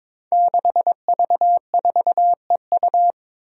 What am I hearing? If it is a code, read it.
Morse code, heard as a keyed tone, about 23 wpm, 700 Hz. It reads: 6V4EU
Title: Five Challenging Letter-Number Combinations 22wpm